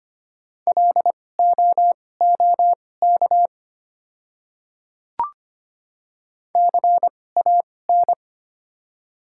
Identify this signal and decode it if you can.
Morse code, heard as a keyed tone, about 25 wpm, spaced as for 18 wpm, 700 Hz. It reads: LOOK CAN
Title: 2x Character Spacing - Top 100 Words - 25wpm